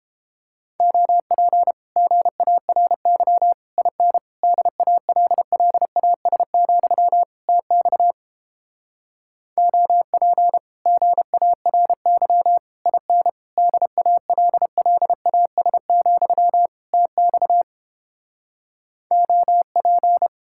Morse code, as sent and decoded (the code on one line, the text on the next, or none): --- .--. / --. .- .-. -.-- / .. -. / -.. .- .-.. .-.. .- ... --..-- / - -..- / --- .--. / --. .- .-. -.-- / .. -. / -.. .- .-.. .-.. .- ... --..-- / - -..- / --- .--.
OP GARY IN DALLAS, TX OP GARY IN DALLAS, TX OP